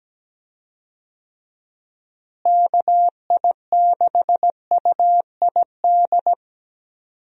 Morse code, heard as a keyed tone, about 17 wpm, 700 Hz. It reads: KI6UID
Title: USA Call Signs - 6 Characters - Repeated 2x - 17wpm